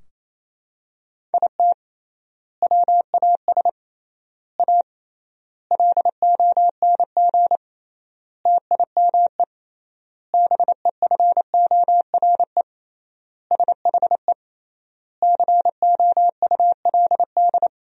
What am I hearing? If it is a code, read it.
Morse code, heard as a keyed tone, about 27 wpm, 700 Hz. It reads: IT WAS A LONG TIME BEFORE SHE COULD
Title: 3x Word-Spacing - Sentences from Top 500 Words - 28wpm